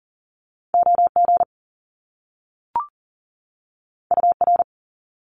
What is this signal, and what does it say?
Morse code, heard as a keyed tone, about 42 wpm, 700 Hz. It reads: OG UR
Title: Top Two Letter Combinations in Words 40wpm